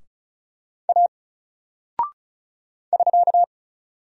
Morse code, heard as a keyed tone, about 36 wpm, 700 Hz. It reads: A <SK>
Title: Single Character (Letter:Number:Punctuation:Prosign) 35wpm